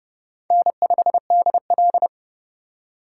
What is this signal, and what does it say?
Morse code, heard as a keyed tone, about 30 wpm, 700 Hz. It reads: N5DL
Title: USA Call Signs - 2x Character Spacing - 30wpm